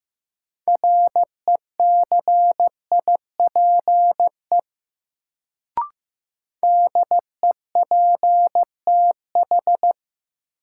Morse code, heard as a keyed tone, about 15 wpm, 700 Hz. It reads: RECIPE DEPTH
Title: Sentences from Top 1200 Words - New Word Drill - 15wpm